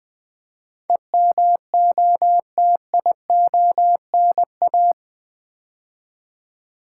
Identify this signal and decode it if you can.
Morse code, heard as a keyed tone, about 20 wpm, 700 Hz. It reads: EMOTIONA
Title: Build A Word - 4 Letters At A Time - 11-Letter Words - 20wpm